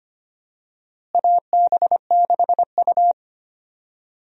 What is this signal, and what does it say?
Morse code, heard as a keyed tone, about 25 wpm, 700 Hz. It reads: AB6U
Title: USA Call Signs - 4 Characters - Repeated 3x - 25wpm